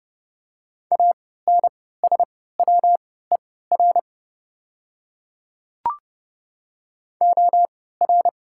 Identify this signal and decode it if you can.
Morse code, heard as a keyed tone, about 30 wpm, spaced as for 17 wpm, 700 Hz. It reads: ANSWER OR
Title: 3x Character Spacing - Top 300 Words - 30wpm